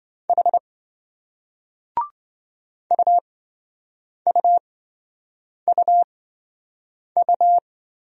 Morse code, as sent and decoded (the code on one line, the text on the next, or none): .... / ..- ..- ..- ..-
H UUUU